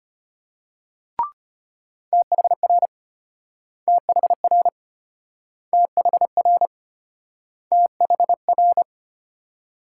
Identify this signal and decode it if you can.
Morse code, heard as a keyed tone, about 32 wpm, 700 Hz. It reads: THR THR THR THR